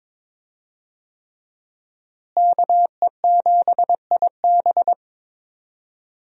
Morse code, heard as a keyed tone, about 21 wpm, 700 Hz. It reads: KE7IB
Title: USA Call Signs - 5 Characters - Repeated 3x - 22wpm